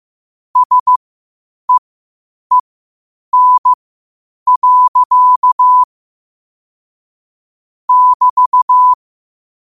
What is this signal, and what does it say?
Morse code, heard as a keyed tone, about 15 wpm, spaced as for 8 wpm, 1000 Hz. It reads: SEEN. <BT>
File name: qotdfiles / 8wpm